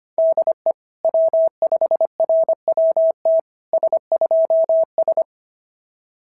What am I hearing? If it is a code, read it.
Morse code, heard as a keyed tone, about 24 wpm, 650 Hz. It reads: DE W5RWT S2S
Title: SOTA QSO Practice 25wpm